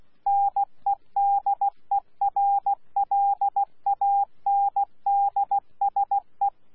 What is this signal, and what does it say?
Morse code, heard as a keyed tone, about 16 wpm, 800 Hz. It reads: NEDERLANDSE
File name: Z PI4VRZ / PI4VRZ 14-jun-2025